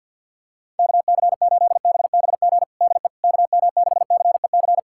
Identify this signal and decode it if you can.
Morse code, heard as a keyed tone, about 48 wpm, 700 Hz. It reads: KC8BBG DE KM6CEC
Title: USA Callsign DE Callsign 50wpm